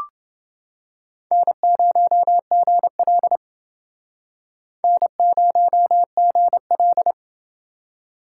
Morse code, heard as a keyed tone, about 30 wpm, 700 Hz. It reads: N0GL N0GL